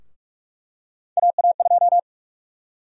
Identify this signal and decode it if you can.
Morse code, heard as a keyed tone, about 46 wpm, 700 Hz. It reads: AA2
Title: USA Call Sign Prefixes - Repeated 2x - 45wpm